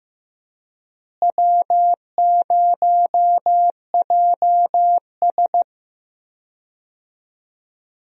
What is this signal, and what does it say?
Morse code, heard as a keyed tone, about 15 wpm, 700 Hz. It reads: W0JS
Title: USA Call Signs - 4 Characters - Repeated 3x - 15wpm